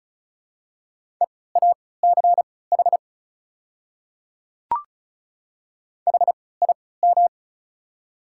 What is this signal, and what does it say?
Morse code, heard as a keyed tone, about 34 wpm, spaced as for 20 wpm, 690 Hz. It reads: EACH HIM
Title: 3x Character Spacing - Top 100 Words - 35wpm